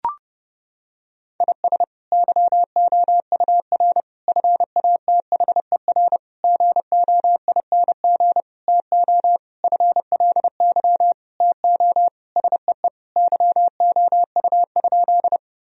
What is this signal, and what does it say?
Morse code, heard as a keyed tone, about 30 wpm, 700 Hz. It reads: IS YOUR FATHER GOING TO FLY TO SEE YOU?
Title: Sentences from Top 1000 Words - Review Part 2 - 30wpm